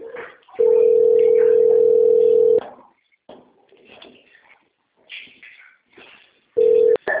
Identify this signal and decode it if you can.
Ringback tone (North America), 440 + 480 Hz.
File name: rg-1600-unknown- - - .18260